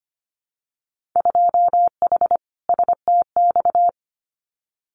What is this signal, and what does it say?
Morse code, heard as a keyed tone, about 25 wpm, 700 Hz. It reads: 2H STX
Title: Winter Field Day Exchanges Encore 25wpm